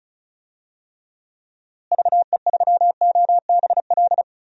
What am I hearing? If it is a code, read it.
Morse code, heard as a keyed tone, about 36 wpm, 700 Hz. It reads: VE3OBL
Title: DXCC-100 Call Signs - 6 Characters - Repeated 2x - 35wpm